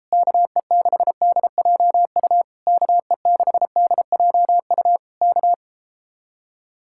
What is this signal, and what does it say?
Morse code, heard as a keyed tone, about 38 wpm, 700 Hz. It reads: KE6DJU KE6DJU K